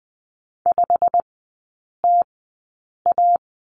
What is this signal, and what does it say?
Morse code, heard as a keyed tone, about 20 wpm, spaced as for 8 wpm, 700 Hz. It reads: 5TA
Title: 184 Lesson 30e - Letter b - Three Characters 20wpm